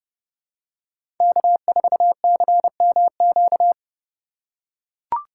Morse code, heard as a keyed tone, about 30 wpm, 700 Hz. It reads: K4CMQ
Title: USA Call Signs - 5 Characters - 4x Character Spacing - 30wpm